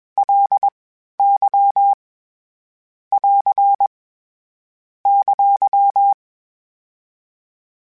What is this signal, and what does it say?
Morse code, heard as a keyed tone, about 21 wpm, 800 Hz. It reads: L Y <AR> !
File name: Basic Sets / Basic Letter Groups Collective7 / LFCYQ@20